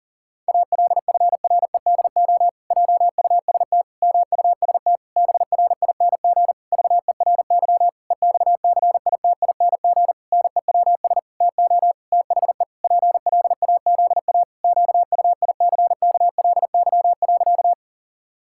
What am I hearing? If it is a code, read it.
Morse code, heard as a keyed tone, about 40 wpm, 700 Hz. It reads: ALFREDO JUST MUST BRING VERY EXCITING NEWS TO THE PLAZA QUICKLY.
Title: Pangram Challenge v2 - 40wpm